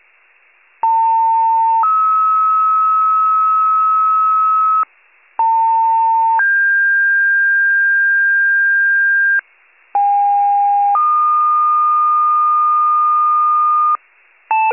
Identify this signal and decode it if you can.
Two-tone page, heard as a one-second tone, then a three-second tone, about 900 then 1290 Hz, about 900 then 1600 Hz, about 810 then 1220 Hz. It is heard from 0.8 to 14.0 s.